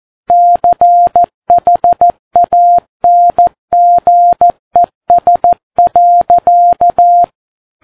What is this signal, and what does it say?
Morse code, heard as a keyed tone, about 14 wpm, 700 Hz. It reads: CHANGES.
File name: Ham Radio Sat